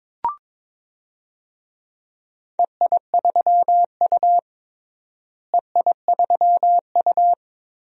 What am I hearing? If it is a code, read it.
Morse code, heard as a keyed tone, about 22 wpm, 700 Hz. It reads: EI3U EI3U
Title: DXCC-100 Call Signs - 4 Characters - Repeated 3x - 22wpm